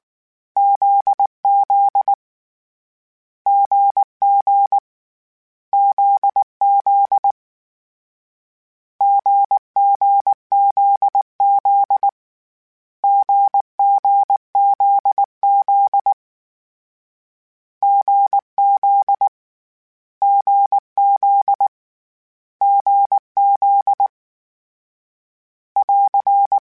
Morse code, heard as a keyed tone, about 19 wpm, 800 Hz. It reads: ZZ GG ZZ GGZZ GGZZ GZ GZ GZ <AR>